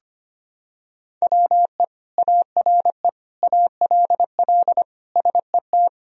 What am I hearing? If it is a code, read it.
Morse code, heard as a keyed tone, about 25 wpm, 700 Hz. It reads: WE ARE ALL SET
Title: Sentences from Top 1000 Words - Review Part 1 - 25wpm